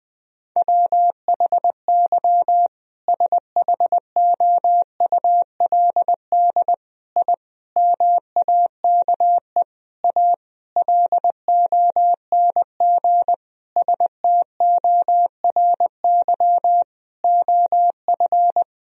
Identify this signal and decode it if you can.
Morse code, heard as a keyed tone, about 21 wpm, 700 Hz. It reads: WHY SHOULD I MAKE A LONG STORY OF
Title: Sentences from Top 1000 Words - Review Part 1 - 20wpm